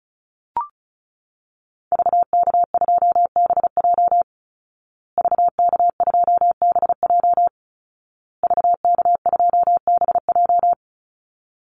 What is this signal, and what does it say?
Morse code, heard as a keyed tone, about 34 wpm, 700 Hz. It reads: VK2BJ VK2BJ VK2BJ
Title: Call Signs Repeated 2x - 35wpm